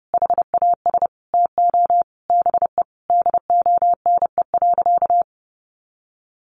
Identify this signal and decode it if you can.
Morse code, heard as a keyed tone, about 30 wpm, 700 Hz. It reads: HAS TO BE DONE.
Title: Sentences from Top 700 Words 30wpm